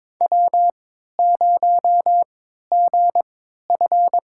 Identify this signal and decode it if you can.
Morse code, heard as a keyed tone, about 23 wpm, spaced as for 13 wpm, 700 Hz. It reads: W0GF
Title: USA Call Signs - 4 Characters - 3x Character Spacing - 22wpm